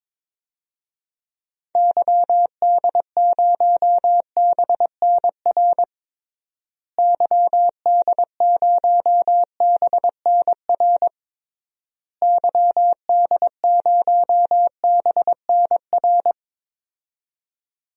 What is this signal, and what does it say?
Morse code, heard as a keyed tone, about 21 wpm, 700 Hz. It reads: YD0BNR YD0BNR YD0BNR
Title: DXCC-100 Call Signs - 6 Characters - Repeated 3x - 22wpm